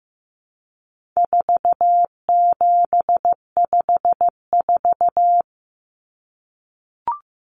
Morse code, heard as a keyed tone, about 15 wpm, 700 Hz. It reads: 4754
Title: Common Cut Numbers - 4-Digit 15wpm